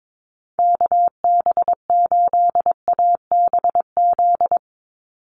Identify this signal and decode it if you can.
Morse code, heard as a keyed tone, about 22 wpm, 700 Hz. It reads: KB8ABZ
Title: USA Call Signs 22wpm